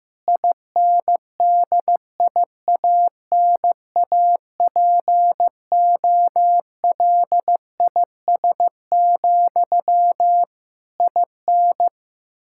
Morse code, heard as a keyed tone, about 15 wpm, 700 Hz. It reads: INDIANAPOLIS, IN
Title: K1USN SST QSO Practice 15wpm